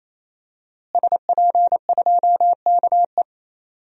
Morse code, heard as a keyed tone, about 28 wpm, 700 Hz. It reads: SP2KE
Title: DXCC-100 Call Signs - 5 Characters - Repeated 3x - 28wpm